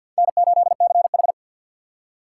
Morse code, heard as a keyed tone, about 53 wpm, 700 Hz. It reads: N8CH
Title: USA Call Signs - 4 Characters - 3x Character Spacing - 50wpm